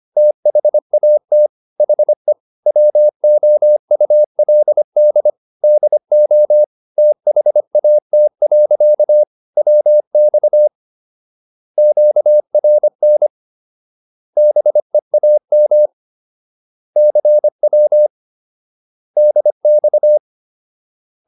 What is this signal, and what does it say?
Morse code, heard as a keyed tone, about 25 wpm, 600 Hz. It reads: THAT HE WOULD DO THAT. WX QRN BEAM CW DX